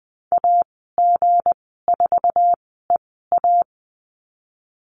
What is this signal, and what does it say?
Morse code, heard as a keyed tone, about 20 wpm, spaced as for 15 wpm, 700 Hz. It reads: AG4EA
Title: 161 Lesson 26g - Letter g - Call Signs 20wpm